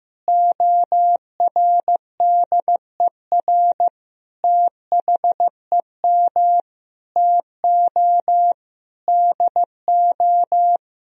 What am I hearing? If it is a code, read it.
Morse code, heard as a keyed tone, about 15 wpm, 700 Hz. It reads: ORDER THEM TO DO
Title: Sentences from Top 1000 Words - Review Part 2 - 15wpm